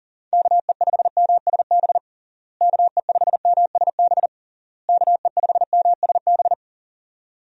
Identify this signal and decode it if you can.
Morse code, heard as a keyed tone, about 37 wpm, 700 Hz. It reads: KE5MSB KE5MSB KE5MSB
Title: USA Call Signs - 6 Characters - Repeated 3x - 40wpm